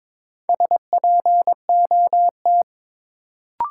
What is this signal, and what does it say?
Morse code, heard as a keyed tone, about 22 wpm, 700 Hz. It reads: SPOT
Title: Sentences from Top 1100 Words - New Word Drill Alt - 22wpm